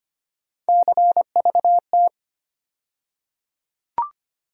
Morse code, heard as a keyed tone, about 25 wpm, 700 Hz. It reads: CVT
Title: USA Call Sign Postfixes - 3 Letters 25wpm